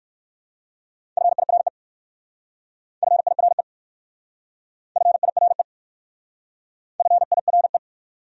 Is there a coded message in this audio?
Yes: Morse code from 1.2 to 7.8 s, heard as a keyed tone, about 59 wpm, 700 Hz. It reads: FIRE FIRE FIRE FIRE